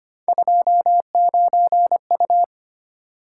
Morse code, heard as a keyed tone, about 25 wpm, 700 Hz. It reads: 29U
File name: Three Letter-Number Combinations - 25wpm